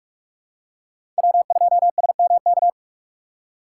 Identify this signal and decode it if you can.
Morse code, heard as a keyed tone, about 48 wpm, 700 Hz. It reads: W2SMK
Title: USA Call Signs - 3x Character Spacing - 45wpm